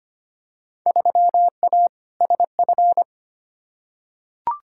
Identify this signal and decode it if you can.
Morse code, heard as a keyed tone, about 25 wpm, 700 Hz. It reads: 3A SF
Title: Field Day Exchanges Encore - 25wpm